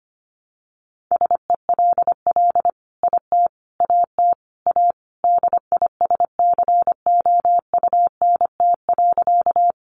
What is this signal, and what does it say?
Morse code, heard as a keyed tone, about 25 wpm, 700 Hz. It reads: SELL IT AT A DISCOUNT.
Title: Sentences from Top 1200 Words 25wpm